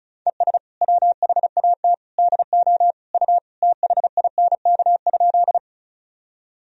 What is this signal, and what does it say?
Morse code, heard as a keyed tone, about 34 wpm, 700 Hz. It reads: ES WHAT DO U THINK?
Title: Sentences from Top 200 Words and Abbreviations 35wpm